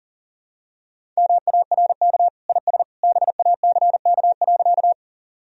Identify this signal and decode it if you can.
Morse code, heard as a keyed tone, about 40 wpm, 700 Hz. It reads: MARK IS BACK.
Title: Sentences from Top 1000 Words - Review Part 1 - 40wpm